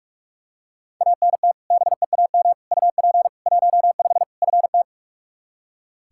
Morse code, heard as a keyed tone, about 46 wpm, 700 Hz. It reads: ANT BEAM UP 15 FT
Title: Practice QSO - Exchange 2 - Info - 45wpm